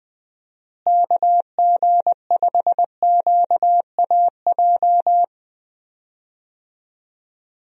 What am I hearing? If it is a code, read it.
Morse code, heard as a keyed tone, about 20 wpm, 700 Hz. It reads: KG5QAJ
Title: USA Call Signs - 6 Characters - Repeated 2x - 20wpm